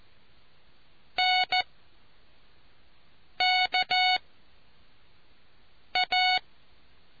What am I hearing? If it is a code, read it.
Morse code, heard as a keyed tone, about 14 wpm, 740 Hz. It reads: N K A